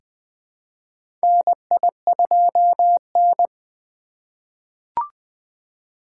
Morse code, heard as a keyed tone, about 20 wpm, 700 Hz. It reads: NI2N
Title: USA Call Signs - 4 Characters - 3x Character Spacing - 20wpm